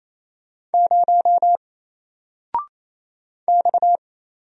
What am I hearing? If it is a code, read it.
Morse code, heard as a keyed tone, about 29 wpm, 700 Hz. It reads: 0 X
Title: Single Character (Letter:Number:Punctuation:Prosign) 28wpm